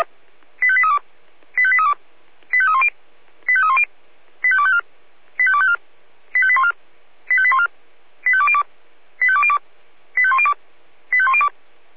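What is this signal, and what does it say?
Five-tone selective call (ZVEI), heard as quick runs of five tones, tones of about 70 ms. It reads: 85832 85832 85310 85310 85254 85254 85814 85814 85202 85202 85102 85102
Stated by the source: ZVEI five-tone call received by SDR.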